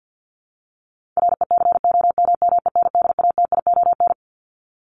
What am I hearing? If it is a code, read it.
Morse code, heard as a keyed tone, about 50 wpm, 700 Hz. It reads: RECOMMENDATION
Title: Top 1251-1500 Words 50wpm